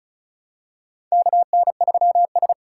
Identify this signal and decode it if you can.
Morse code, heard as a keyed tone, about 36 wpm, 700 Hz. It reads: KN3S
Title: USA Call Signs - 4 Characters - 2x Character Spacing - 35wpm